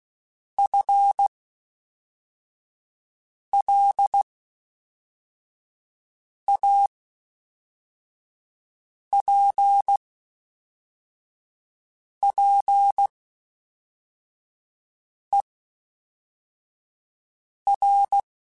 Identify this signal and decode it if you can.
Morse code, heard as a keyed tone, about 16 wpm, spaced as for 4 wpm, 800 Hz. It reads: FLAPPER
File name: Spaced / FL Runs sp mp3s / FL Runs sp @5